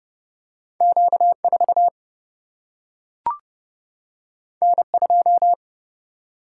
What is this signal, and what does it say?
Morse code, heard as a keyed tone, about 30 wpm, 700 Hz. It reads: Q 4 N 2
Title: Single-Letter Single-Number 30wpm